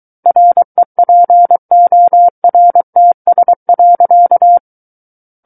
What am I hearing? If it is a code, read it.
Morse code, heard as a keyed tone, about 23 wpm, 700 Hz. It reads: REPORTS.
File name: Book One / 14-Chapter 14 - 23 WPM